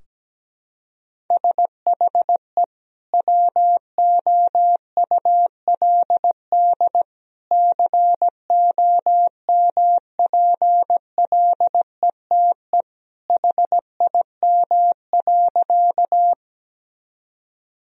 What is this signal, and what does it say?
Morse code, heard as a keyed tone, about 17 wpm, 700 Hz. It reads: SHE WOULD COMPLETE HIM.
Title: Sentences from Top 1000 Words - Review Part 2 - 17wpm